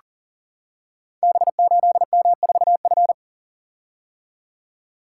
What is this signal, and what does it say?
Morse code, heard as a keyed tone, about 42 wpm, 690 Hz. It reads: B8M4F
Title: Code Groups - 5 Letter:Numbers - 40wpm